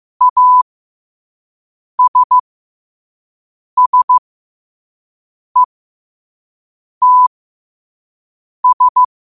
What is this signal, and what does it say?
Morse code, heard as a keyed tone, about 15 wpm, spaced as for 5 wpm, 1000 Hz. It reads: ASSETS
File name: qotdfiles / 5wpm